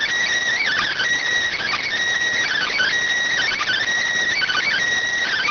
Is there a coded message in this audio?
Yes: Emergency Alert System (SAME) tones from 0.6 to 4.7 s, heard as harsh buzzing bursts.